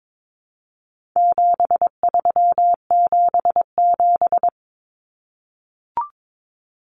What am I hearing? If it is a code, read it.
Morse code, heard as a keyed tone, about 22 wpm, 700 Hz. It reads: 7377
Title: Common Cut Numbers - 4-Digit 22wpm